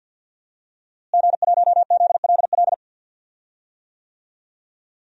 Morse code, heard as a keyed tone, about 48 wpm, 700 Hz. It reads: G1ZLL
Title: DXCC-100 Call Signs - 5 Characters - Repeated 3x - 50wpm